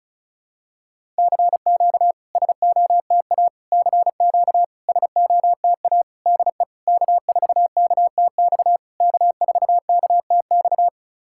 Morse code, heard as a keyed tone, about 36 wpm, 690 Hz. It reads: CQ SOTA CQ SOTA DE K4KTX K4KTX
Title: SOTA QSO Practice 35wpm